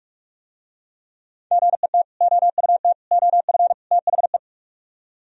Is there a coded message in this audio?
Yes: Morse code from 1.5 to 4.4 s, heard as a keyed tone, about 46 wpm, 690 Hz. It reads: GET OUT OF THE